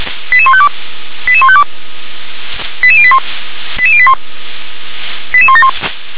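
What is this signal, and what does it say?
Five-tone selective call (ZVEI), heard as quick runs of five tones, tones of about 70 ms. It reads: 88152 88152 88081 88081 88171
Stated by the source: received on SDR, ZVEI five-tone selective call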